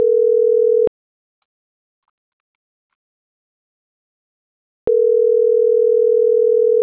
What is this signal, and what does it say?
Ringback tone (North America), 440 + 480 Hz.